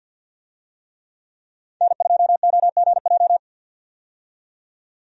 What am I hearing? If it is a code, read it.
Morse code, heard as a keyed tone, about 50 wpm, 700 Hz. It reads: N2OGJ
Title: USA Call Signs - 5 Characters - 50wpm